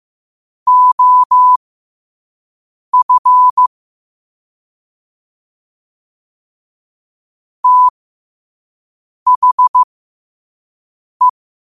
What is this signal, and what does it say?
Morse code, heard as a keyed tone, about 15 wpm, spaced as for 5 wpm, 1000 Hz. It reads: OF THE
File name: qotdfiles / 5wpm